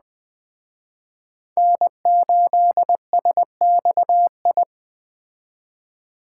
Morse code, heard as a keyed tone, about 19 wpm, 700 Hz. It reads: N8SXI